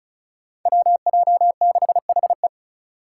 Morse code, heard as a keyed tone, about 34 wpm, 700 Hz. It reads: WJ6HE
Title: Five Challenging Letter-Number Combinations 35wpm